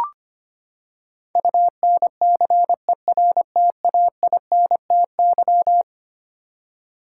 Morse code, heard as a keyed tone, about 25 wpm, 700 Hz. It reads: UNCERTAINTY
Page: morse-code-ninja-practice-25wpm